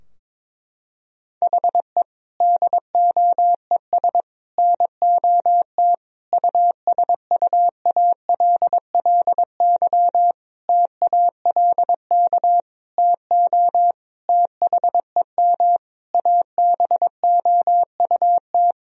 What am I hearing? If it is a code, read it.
Morse code, heard as a keyed tone, about 22 wpm, 700 Hz. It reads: HE DOES NOT USUALLY TALK TO THEM ABOUT
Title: Sentences from Top 1000 Words - Review Part 3 - 22wpm